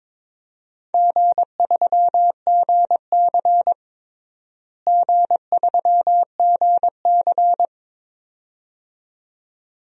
Morse code, heard as a keyed tone, about 22 wpm, 700 Hz. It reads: G3GC G3GC
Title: DXCC-100 Call Signs - 4 Characters - Repeated 2x - 22wpm